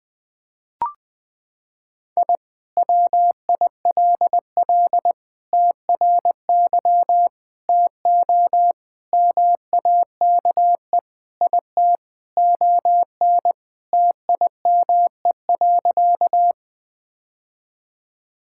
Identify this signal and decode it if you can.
Morse code, heard as a keyed tone, about 21 wpm, 700 Hz. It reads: I WILL TRY TO MAKE IT ON TIME.
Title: Sentences with 3-Word Phrases - 20wpm